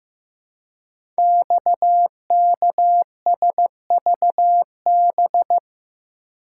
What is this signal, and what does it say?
Morse code, heard as a keyed tone, about 15 wpm, 700 Hz. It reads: XKSVB
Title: Code Groups - 5 Letters - 15wpm